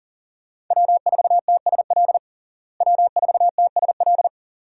Morse code, heard as a keyed tone, about 40 wpm, 700 Hz. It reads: W4TSL W4TSL
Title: USA Call Signs - 5 Characters - Repeated 2x - 40wpm